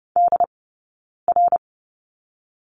Morse code, heard as a keyed tone, about 30 wpm, spaced as for 9 wpm, 690 Hz. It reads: DR
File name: Lesson 18d - Letter m - Two Characters 30wpm